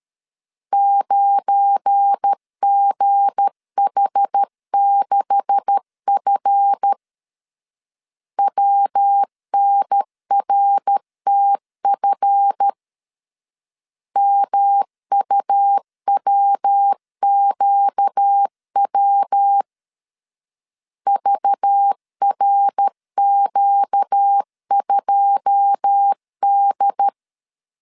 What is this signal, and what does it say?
Morse code, heard as a keyed tone, about 13 wpm, 800 Hz. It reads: 9GH6F WNRTF MUWQW VRQ2D
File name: CW / DET / 1 9 1500C S B 12wpm 3819